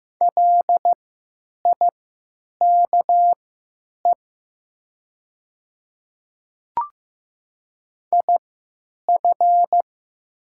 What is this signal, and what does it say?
Morse code, heard as a keyed tone, about 15 wpm, spaced as for 9 wpm, 690 Hz. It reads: LIKE IF
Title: 3x Character Spacing - Top 100 Words - 15wpm